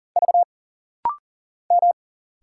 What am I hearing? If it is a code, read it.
Morse code, heard as a keyed tone, about 40 wpm, 700 Hz. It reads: V M
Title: Single Letters - Rapid-Fire 40wpm